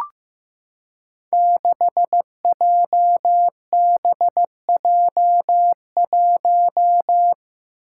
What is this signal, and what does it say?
Morse code, heard as a keyed tone, about 15 wpm, 700 Hz. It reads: 6JBJ1
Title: Five Challenging Letter-Number Combinations 15wpm